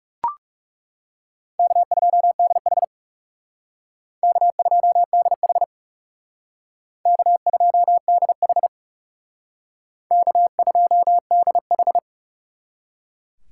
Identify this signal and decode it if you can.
Morse code, heard as a keyed tone, about 36 wpm, 700 Hz. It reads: K2DH K2DH K2DH K2DH
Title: Speed Racing - USA Call Signs 30wpm